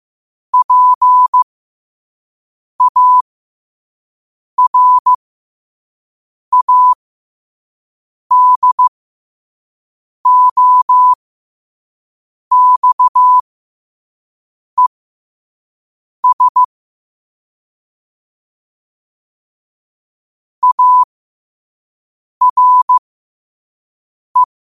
Morse code, heard as a keyed tone, about 15 wpm, spaced as for 5 wpm, 1000 Hz. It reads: PARADOXES ARE
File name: qotdfiles / 5wpm